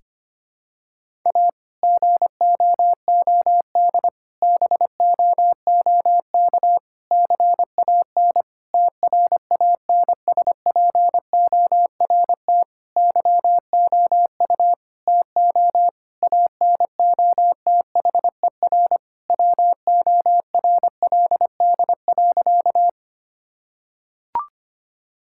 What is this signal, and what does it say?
Morse code, heard as a keyed tone, about 25 wpm, 700 Hz. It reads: A GOOD BOOK CAN TRANSPORT YOU TO ANOTHER WORLD.
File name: practice / 25wpm / Difficult Sentences with Hints - 25wpm